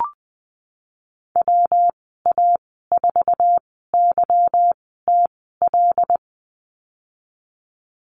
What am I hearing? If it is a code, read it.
Morse code, heard as a keyed tone, about 20 wpm, spaced as for 15 wpm, 700 Hz. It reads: WA4YTL
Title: 147 Lesson 24g - Letter y - Call Signs 20wpm